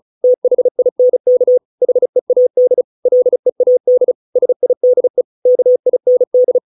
Morse code, heard as a keyed tone, about 34 wpm, 500 Hz. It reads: THINK HEAD LEAD SIDE KIND